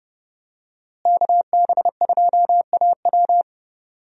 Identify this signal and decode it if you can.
Morse code, heard as a keyed tone, about 30 wpm, 700 Hz. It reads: KB2AW
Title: USA Call Signs - 5 Characters - Repeated 3x - 30wpm